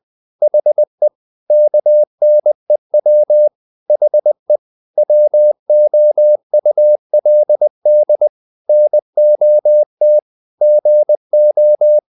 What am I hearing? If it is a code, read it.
Morse code, heard as a keyed tone, about 20 wpm, 600 Hz. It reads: HE KNEW HE WOULD NOT GO
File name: the old man and the sea part 2 20wpm